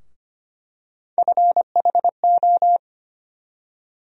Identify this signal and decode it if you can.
Morse code, heard as a keyed tone, about 25 wpm, 700 Hz. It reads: FHO
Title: USA Call Sign Postfixes - 3 Letters 25wpm